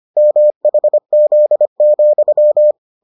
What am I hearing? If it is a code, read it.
Morse code, heard as a keyed tone, about 25 wpm, 600 Hz. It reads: MHZ,